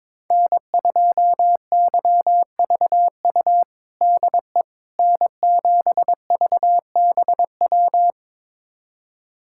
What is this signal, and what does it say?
Morse code, heard as a keyed tone, about 22 wpm, 700 Hz. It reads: N2YVU DE N7VBW
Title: USA Callsign DE Callsign 22wpm